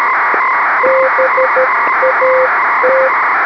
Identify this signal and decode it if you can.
Morse code, heard as a keyed tone, about 13 wpm, 520 Hz, with noise. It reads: BAT
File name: arissat-1 OK-ISS 06h54utc by dk3wn